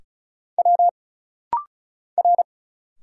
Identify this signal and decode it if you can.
Morse code, heard as a keyed tone, about 36 wpm, 700 Hz. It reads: W R
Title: Single Letters - Rapid-Fire 35wpm